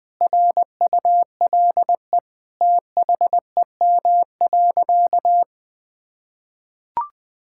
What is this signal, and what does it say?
Morse code, heard as a keyed tone, about 21 wpm, 700 Hz. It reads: RULE THEM.
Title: Sentences from Top 1600 Words 20wpm